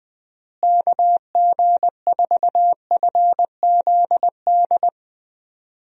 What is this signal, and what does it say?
Morse code, heard as a keyed tone, about 19 wpm, 700 Hz. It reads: KG4FZD
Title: USA Call Signs - 6 Characters - Repeated 2x - 20wpm